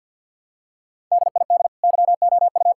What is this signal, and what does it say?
Morse code, heard as a keyed tone, about 50 wpm, 700 Hz. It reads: DID YOU
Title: Sentences from Top 1000 Words - Review Part 5 - 50wpm